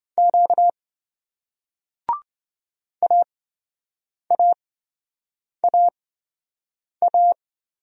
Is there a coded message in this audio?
Yes: Morse code from 0.2 to 7.3 s, heard as a keyed tone, about 26 wpm, spaced as for 7 wpm, 700 Hz. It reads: Q AAAA